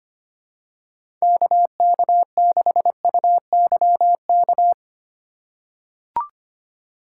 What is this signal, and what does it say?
Morse code, heard as a keyed tone, about 25 wpm, 700 Hz. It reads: KK6UYK
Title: USA Call Signs - 6 Characters - 2x Character Spacing - 25wpm